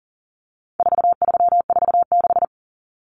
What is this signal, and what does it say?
Morse code, heard as a keyed tone, about 40 wpm, 700 Hz. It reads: 4346
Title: Numbers - 4-Digit 40wpm